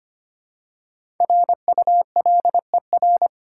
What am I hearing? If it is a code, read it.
Morse code, heard as a keyed tone, about 25 wpm, 700 Hz. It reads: RULER
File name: practice / 25wpm / Easy Words with Hints - 25wpm